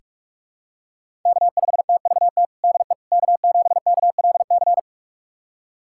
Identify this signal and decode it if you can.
Morse code, heard as a keyed tone, about 46 wpm, 700 Hz. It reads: K5TVT DE K7KLC
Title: USA Callsign DE Callsign - 5 Characters - 45wpm